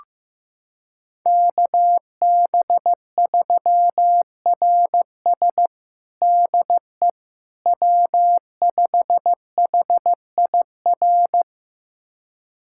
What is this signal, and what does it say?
Morse code, heard as a keyed tone, about 15 wpm, 700 Hz. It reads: KB3RS DE W5HIR
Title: USA Callsign DE Callsign 15wpm